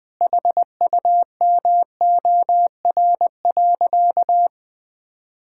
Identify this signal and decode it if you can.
Morse code, heard as a keyed tone, about 21 wpm, 700 Hz. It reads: HUMOR.
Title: Sentences with 2-Word Phrases - 20wpm